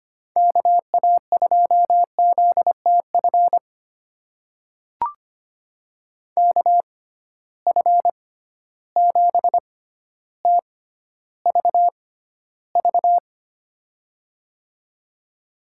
Morse code, heard as a keyed tone, about 25 wpm, spaced as for 13 wpm, 700 Hz. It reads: KA2ZTF KF7TVV
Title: USA Call Signs - 6 Characters - 6x Character Spacing - 25wpm